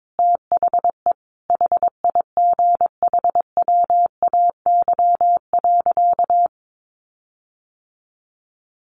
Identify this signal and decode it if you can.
Morse code, heard as a keyed tone, about 22 wpm, 700 Hz. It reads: THE HIGHWAY.
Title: Sentences from Top 1400 Words 22wpm